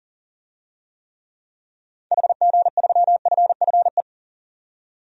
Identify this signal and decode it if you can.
Morse code, heard as a keyed tone, about 40 wpm, 700 Hz. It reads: HG3FFE
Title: DXCC-100 Call Signs - 6 Characters - 40wpm